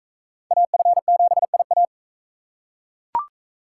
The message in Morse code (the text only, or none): AF7IA